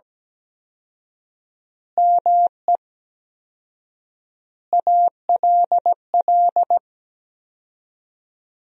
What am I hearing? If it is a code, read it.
Morse code, heard as a keyed tone, about 17 wpm, 700 Hz. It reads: ME ALL
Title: 4x Word-Spacing - Sentences from Top 400 Words - 17wpm